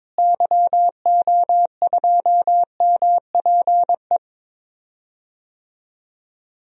Morse code, heard as a keyed tone, about 22 wpm, 700 Hz. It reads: YO2MPE
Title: DXCC-100 Call Signs - 6 Characters - Repeated 2x - 22wpm